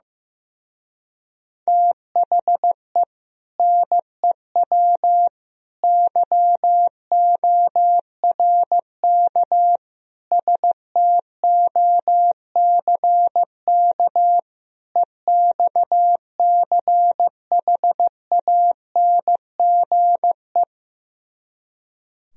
Morse code, heard as a keyed tone, about 15 wpm, 700 Hz. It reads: THE NEW YORK STOCK EXCHANGE
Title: Speed-Racing - Sets of 5 Words 15wpm